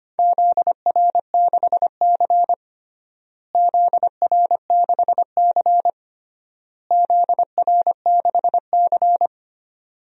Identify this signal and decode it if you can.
Morse code, heard as a keyed tone, about 25 wpm, 700 Hz. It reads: ZR6C ZR6C ZR6C
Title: DXCC-100 Call Signs - 4 Characters - Repeated 3x - 25wpm